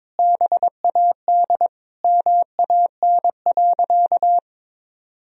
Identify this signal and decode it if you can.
Morse code, heard as a keyed tone, about 22 wpm, 700 Hz. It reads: BAD MAN.
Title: Sentences from Top 1000 Words - Review Part 1 - 22wpm